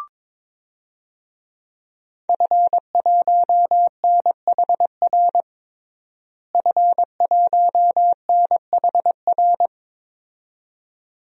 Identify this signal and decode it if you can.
Morse code, heard as a keyed tone, about 22 wpm, 700 Hz. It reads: F1NHR F1NHR
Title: DXCC-100 Call Signs - 5 Characters - Repeated 2x - 22wpm